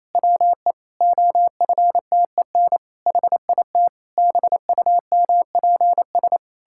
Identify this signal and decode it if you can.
Morse code, heard as a keyed tone, about 28 wpm, 700 Hz. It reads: WE OFTEN HIT BUMPS
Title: Pangram Challenge v2 - 28wpm